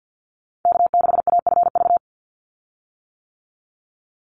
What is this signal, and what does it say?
Morse code, heard as a keyed tone, about 50 wpm, 700 Hz. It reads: K6AFV
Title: USA Call Signs 50wpm